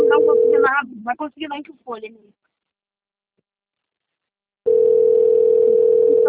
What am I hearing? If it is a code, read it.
Ringback tone (North America), 440 + 480 Hz, with speech.
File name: out- - - - - .20873